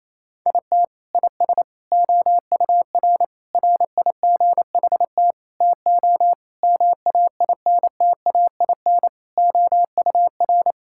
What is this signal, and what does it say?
Morse code, heard as a keyed tone, about 28 wpm, 700 Hz. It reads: IT IS OUR RIGHT TO MAINTAIN OUR
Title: Sentences from Top 1000 Words - Review Part 5 - 28wpm